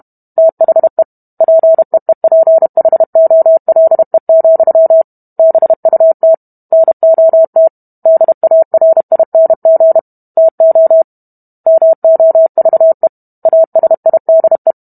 Morse code, heard as a keyed tone, about 32 wpm, 650 Hz. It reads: THE PEEPHOLE, BUT NOT DARING TO MOVE ASIDE